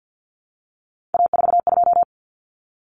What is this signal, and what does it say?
Morse code, heard as a keyed tone, about 50 wpm, 700 Hz. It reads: A42
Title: Common Cut Numbers - 3-Digit 50wpm